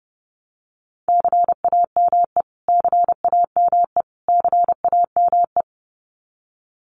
Morse code, heard as a keyed tone, about 30 wpm, 700 Hz. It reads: CAME CAME CAME
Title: Top 300 Words Repeated 30wpm